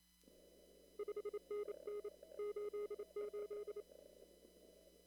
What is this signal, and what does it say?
Morse code, heard as a keyed tone, about 27 wpm, 420 Hz. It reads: 5NN 88
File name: CW / kt5j wpx cw-op-k5pi